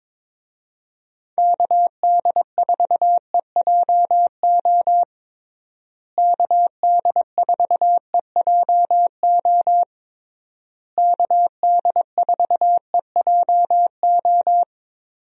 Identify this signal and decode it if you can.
Morse code, heard as a keyed tone, about 22 wpm, 700 Hz. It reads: KD4EJO KD4EJO KD4EJO
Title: USA Call Signs - 6 Characters - Repeated 3x - 22wpm